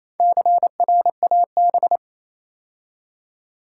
Morse code, heard as a keyed tone, about 27 wpm, 700 Hz. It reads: CRAB
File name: Easy Words - Speed-Racing with Hints - 22wpm